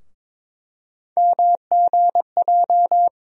Morse code, heard as a keyed tone, about 22 wpm, 700 Hz. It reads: MGJ
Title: USA Call Sign Postfixes - 3 Letters 22wpm